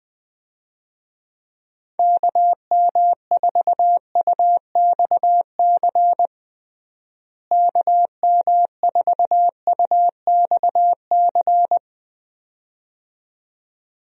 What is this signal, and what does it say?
Morse code, heard as a keyed tone, about 20 wpm, 700 Hz. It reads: KM4UXC KM4UXC
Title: USA Call Signs - 6 Characters - Repeated 2x - 20wpm